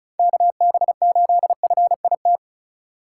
Morse code, heard as a keyed tone, about 36 wpm, 700 Hz. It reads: KB8FIT
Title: USA Call Signs - 6 Characters - 2x Character Spacing - 35wpm